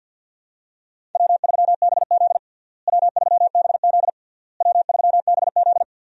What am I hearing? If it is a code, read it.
Morse code, heard as a keyed tone, about 48 wpm, 700 Hz. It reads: W3BZ W3BZ W3BZ
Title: USA Call Signs - 4 Characters - Repeated 3x - 50wpm